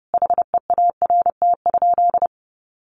Morse code, heard as a keyed tone, about 30 wpm, 700 Hz. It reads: HEART?
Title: Sentences from Top 1200 Words 30wpm